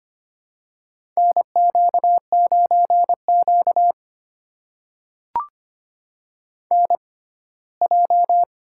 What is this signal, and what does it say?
Morse code, heard as a keyed tone, about 25 wpm, spaced as for 17 wpm, 700 Hz. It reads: NQ9Q NJ
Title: USA Call Signs - 4 Characters - 6x Character Spacing - 25wpm